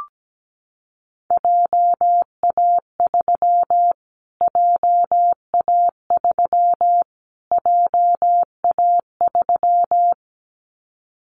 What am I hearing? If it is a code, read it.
Morse code, heard as a keyed tone, about 17 wpm, 700 Hz. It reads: JA3 JA3 JA3
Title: Callsign Prefixes for Top 25 DXCC - Repeated 3x - 17wpm